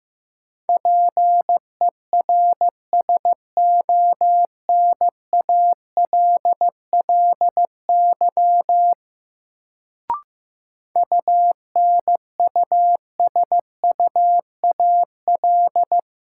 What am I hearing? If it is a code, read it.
Morse code, heard as a keyed tone, about 15 wpm, 700 Hz. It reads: PERSONALLY UNUSUAL
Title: Sentences from Top 1400 Words - New Word Drill - 15wpm